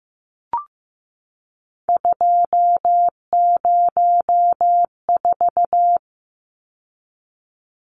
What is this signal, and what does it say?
Morse code, heard as a keyed tone, about 15 wpm, 700 Hz. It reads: 204
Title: Numbers - 3-Digit 15wpm